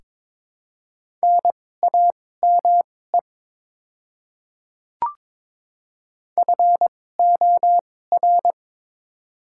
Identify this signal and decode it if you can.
Morse code, heard as a keyed tone, about 22 wpm, spaced as for 16 wpm, 700 Hz. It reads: NAME FOR
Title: 2x Character Spacing - Top 200 Words - 22wpm